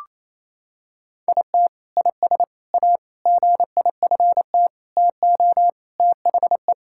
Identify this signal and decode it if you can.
Morse code, heard as a keyed tone, about 28 wpm, 700 Hz. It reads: IT IS A GIFT TO THE
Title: Sentences from Top 1000 Words - Review Part 6 - 28wpm